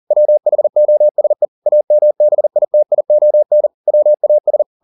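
Morse code, heard as a keyed tone, about 40 wpm, 600 Hz. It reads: WHOSE AMBITION WAS
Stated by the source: Morse code at 40 wpm, practice text, CW med Gnister.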